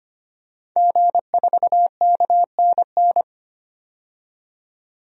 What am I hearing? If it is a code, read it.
Morse code, heard as a keyed tone, about 25 wpm, 690 Hz. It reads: G4KNN
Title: DXCC-100 Call Signs - 5 Characters - 25wpm